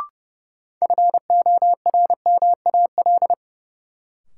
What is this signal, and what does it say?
Morse code, heard as a keyed tone, about 30 wpm, 700 Hz. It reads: FORMAL
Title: Sentences from Top 1300 Words - New Word Drill - 30wpm